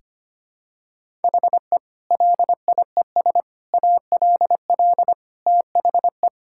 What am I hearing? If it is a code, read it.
Morse code, heard as a keyed tone, about 25 wpm, 700 Hz. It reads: HE LIES ALL THE
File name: practice / 25wpm / Sentences with 3-Word Phrases - 25wpm